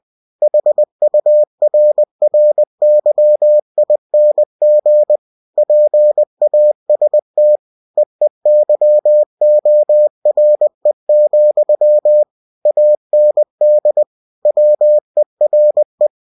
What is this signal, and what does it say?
Morse code, heard as a keyed tone, about 20 wpm, 600 Hz. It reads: HURRYING PAST EEYORE, AND WERE
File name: winnie-20wpm